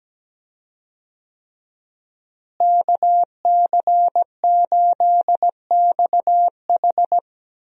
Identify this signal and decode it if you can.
Morse code, heard as a keyed tone, about 17 wpm, 700 Hz. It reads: KC8XH
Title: USA Call Signs - 5 Characters - Repeated 3x - 17wpm